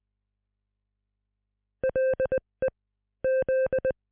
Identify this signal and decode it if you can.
Morse code, heard as a keyed tone, about 20 wpm, spaced as for 18 wpm, 520 Hz. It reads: LE Z